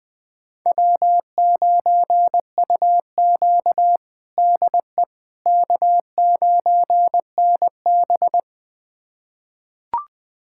Morse code, heard as a keyed tone, about 20 wpm, 700 Hz. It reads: W9UQ DE K9NB
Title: USA Callsign DE Callsign - 4 Characters - 20wpm